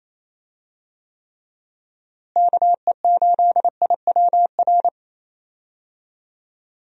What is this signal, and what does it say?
Morse code, heard as a keyed tone, about 28 wpm, 700 Hz. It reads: KE8IWR
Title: USA Call Signs - 6 Characters - Repeated 2x - 28wpm